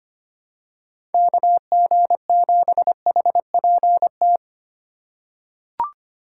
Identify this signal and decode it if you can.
Morse code, heard as a keyed tone, about 25 wpm, 700 Hz. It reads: KG7HPT
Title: USA Call Signs - 6 Characters - 4x Character Spacing - 25wpm